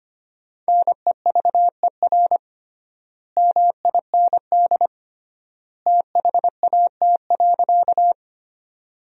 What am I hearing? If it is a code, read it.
Morse code, heard as a keyed tone, about 25 wpm, 700 Hz. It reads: NEVER MIND THAT.
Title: 3x Word-Spacing - Sentences from Top 500 Words - 25wpm